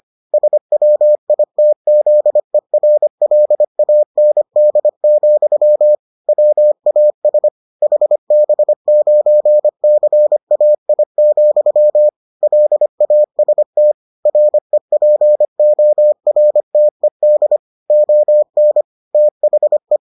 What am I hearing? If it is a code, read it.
Morse code, heard as a keyed tone, about 25 wpm, 600 Hz. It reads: SWITZERLAND, WAS HB9CAI, LAST REPORTED ON THE